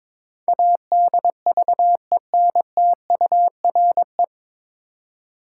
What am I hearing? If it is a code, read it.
Morse code, heard as a keyed tone, about 22 wpm, 690 Hz. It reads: ADVENTURE
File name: 9 Letter Words - First 3 Letters Given - 22wpm